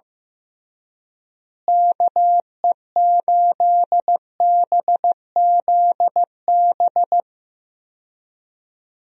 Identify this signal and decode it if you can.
Morse code, heard as a keyed tone, about 15 wpm, 700 Hz. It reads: KE8BZB
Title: USA Call Signs - 6 Characters - Repeated 3x - 15wpm